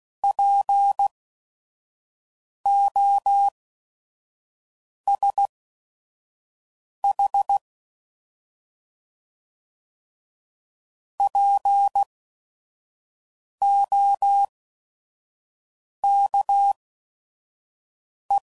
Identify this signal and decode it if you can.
Morse code, heard as a keyed tone, about 16 wpm, spaced as for 5 wpm, 800 Hz. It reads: POSH POKE
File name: Prefixes / By First Letter / Ps